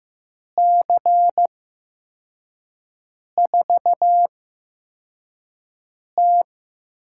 Morse code, heard as a keyed tone, about 15 wpm, spaced as for 4 wpm, 700 Hz. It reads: C4T
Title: USA Call Signs - 5 Characters - 8x Character Spacing - 15wpm